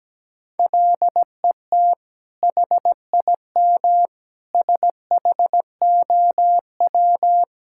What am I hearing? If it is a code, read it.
Morse code, heard as a keyed tone, about 17 wpm, 700 Hz. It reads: LET HIM SHOW
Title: Sentences from Top 1600 Words 17wpm